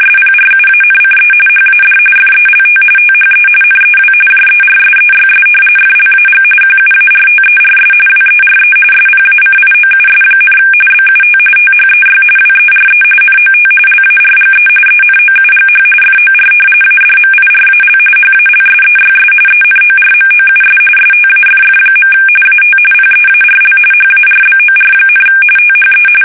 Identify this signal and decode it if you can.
RTTY, about 45.45 baud. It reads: SPRUANCE HAD FELT THE NEW JERSEYS FLAG QUARTERS WERE SPACIOUS, BUT THEY WERE NOT LARGE ENOUGH FOR ADMIRAL HALSEYS ENORMOUS STAFF. AT PEARL HARBOR, THE
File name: fsk850 / battleshipnj